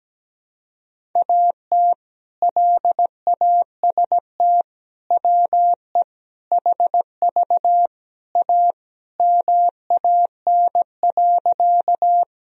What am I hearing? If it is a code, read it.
Morse code, heard as a keyed tone, about 17 wpm, 700 Hz. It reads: AT LAST WE HV A MAN.
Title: Sentences from Top 300 Words and Abbreviations 17wpm